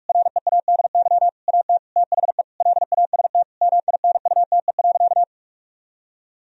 Morse code, heard as a keyed tone, about 46 wpm, 700 Hz. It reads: READY AT THE LAST MINUTE.
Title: Sentences from Top 1000 Words - Review Part 2 - 45wpm